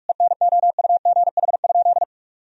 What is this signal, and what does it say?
Morse code, heard as a keyed tone, about 41 wpm, 700 Hz. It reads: ENOUGH?
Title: Sentences from Top 1000 Words - Review Part 2 - 45wpm